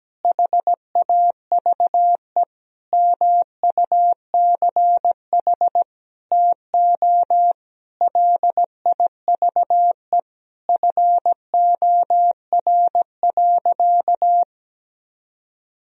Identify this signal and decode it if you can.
Morse code, heard as a keyed tone, about 17 wpm, 700 Hz. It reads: HAVE MUCH TO LIVE FOR.
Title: Sentences from Top 1000 Words - Review Part 3 - 17wpm